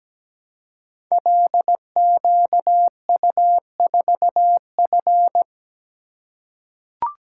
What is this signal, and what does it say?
Morse code, heard as a keyed tone, about 17 wpm, 700 Hz. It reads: LQU4F
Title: Code Groups - 5 Letter:Numbers - 17wpm